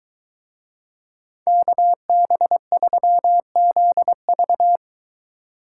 Morse code, heard as a keyed tone, about 23 wpm, 700 Hz. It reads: KB3ZV